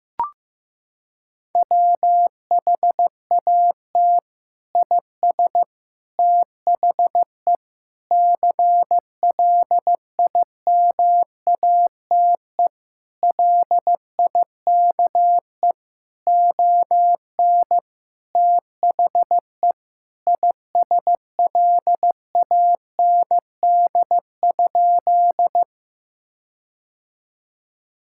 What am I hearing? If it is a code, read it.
Morse code, heard as a keyed tone, about 15 wpm, 700 Hz. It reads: WHAT IS THE CLIMATE LIKE ON THE ISLAND?
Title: Sentences from Top 1600 Words 15wpm